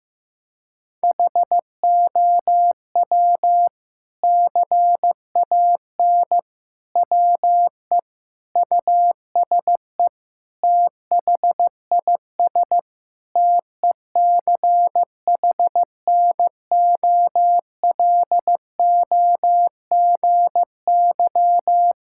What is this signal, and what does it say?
Morse code, heard as a keyed tone, about 15 wpm, 700 Hz. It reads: HOW CAN WE USE THIS TECHNOLOGY
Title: Sentences from Top 1000 Words - Review Part 5 - 15wpm